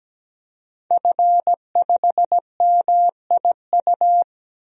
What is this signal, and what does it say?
Morse code, heard as a keyed tone, about 17 wpm, 700 Hz. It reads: F5MIU
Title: DXCC-100 Call Signs - 5 Characters - Repeated 2x - 17wpm